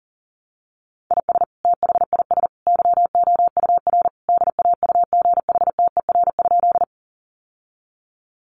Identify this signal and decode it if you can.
Morse code, heard as a keyed tone, about 40 wpm, 700 Hz. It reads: IS THIS YOUR DAUGHTER?
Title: Sentences from Top 1500 Words 40wpm